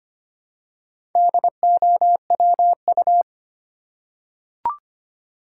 Morse code, heard as a keyed tone, about 25 wpm, 700 Hz. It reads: DOWU
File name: practice / 25wpm / Four Letter-Number Combinations - 25wpm